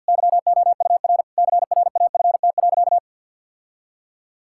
Morse code, heard as a keyed tone, about 50 wpm, 700 Hz. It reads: YOUR CRAFT.
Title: Sentences from Top 1000 Words - Review Part 6 - 50wpm